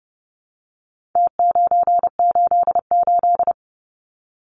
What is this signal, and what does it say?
Morse code, heard as a keyed tone, about 30 wpm, 700 Hz. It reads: T988
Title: Numbers - Contest Style - 4-Digit 30wpm